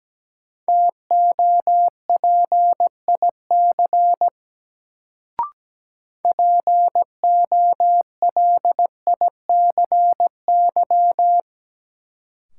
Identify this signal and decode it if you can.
Morse code, heard as a keyed tone, about 17 wpm, 700 Hz. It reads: TOPIC POLICY
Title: Sentences from Top 1200 Words - New Word Drill Alt - 17wpm